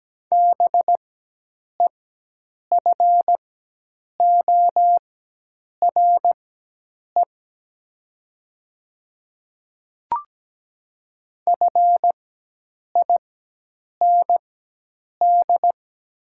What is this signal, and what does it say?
Morse code, heard as a keyed tone, about 17 wpm, spaced as for 8 wpm, 700 Hz. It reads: BEFORE FIND
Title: 4x Character Spacing - Top 500 Words - 17wpm